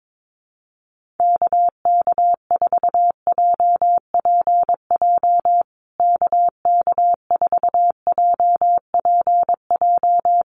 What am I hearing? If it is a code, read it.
Morse code, heard as a keyed tone, about 22 wpm, 700 Hz. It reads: KK4JPJ KK4JPJ
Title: USA Call Signs Repeated 2x 22wpm